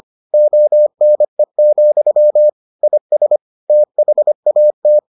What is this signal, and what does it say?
Morse code, heard as a keyed tone, about 25 wpm, 600 Hz. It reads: ONE, IS THAT